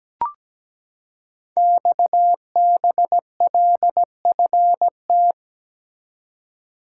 Morse code, heard as a keyed tone, about 17 wpm, 700 Hz. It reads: XBLFT
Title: Code Groups - 5 Letters - 17wpm